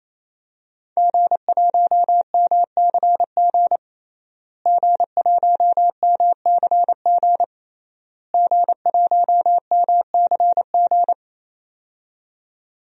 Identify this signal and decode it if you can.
Morse code, heard as a keyed tone, about 27 wpm, 700 Hz. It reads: G1MCG G1MCG G1MCG
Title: DXCC-100 Call Signs - 5 Characters - Repeated 3x - 28wpm